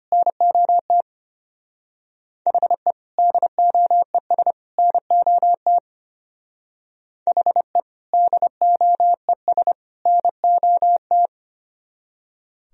Morse code, heard as a keyed tone, about 31 wpm, 700 Hz. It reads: NOT HE DOES NOT HE DOES NOT
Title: Speed-Racing - Sets of 3 Words 25wpm